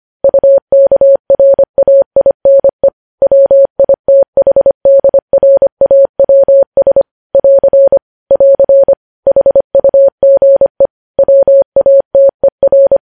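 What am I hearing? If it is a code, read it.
Morse code, heard as a keyed tone, about 25 wpm, 550 Hz. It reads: UKRAINE WITHDRAWS <AR> <AR> HUGE WATER
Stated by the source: Morse practice sending news headlines at 25 wpm.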